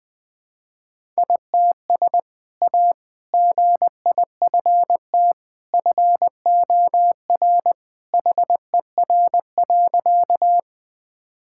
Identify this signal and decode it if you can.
Morse code, heard as a keyed tone, about 21 wpm, 700 Hz. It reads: ITS A GIFT FOR HER.
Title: Sentences with 2-Word Phrases - Spoken Beginning - 20wpm